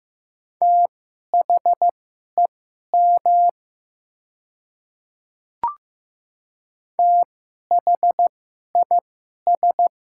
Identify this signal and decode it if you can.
Morse code, heard as a keyed tone, about 15 wpm, spaced as for 11 wpm, 700 Hz. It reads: THEM THIS
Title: 2x Character Spacing - Top 100 Words - 15wpm